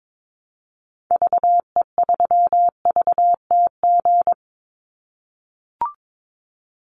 Morse code, heard as a keyed tone, about 22 wpm, 700 Hz. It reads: VE3VTG
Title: Call Signs - 22wpm